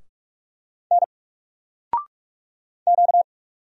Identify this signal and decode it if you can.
Morse code, heard as a keyed tone, about 44 wpm, 700 Hz. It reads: N Q
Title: Single Character (Letter:Number:Punctuation:Prosign) 45wpm